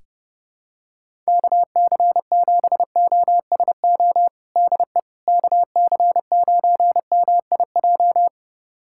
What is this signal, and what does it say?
Morse code, heard as a keyed tone, about 30 wpm, 700 Hz. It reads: KC7OSO DE KC9MIJ
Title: USA Callsign DE Callsign - 6 Characters - 30wpm